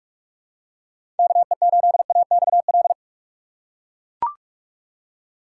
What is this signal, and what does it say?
Morse code, heard as a keyed tone, about 44 wpm, 700 Hz. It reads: KE8AXL
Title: USA Call Signs - 6x Character Spacing - 45wpm